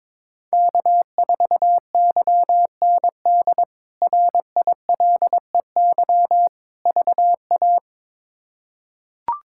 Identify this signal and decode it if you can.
Morse code, heard as a keyed tone, about 21 wpm, 700 Hz. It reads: K4YND RILEY VA
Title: K1USN SST QSO Practice - 22wpm